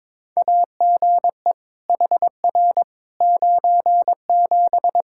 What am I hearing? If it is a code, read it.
Morse code, heard as a keyed tone, about 21 wpm, 700 Hz. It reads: AGE HR 97
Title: Practice QSO - Exchange 3 - More Info - 22wpm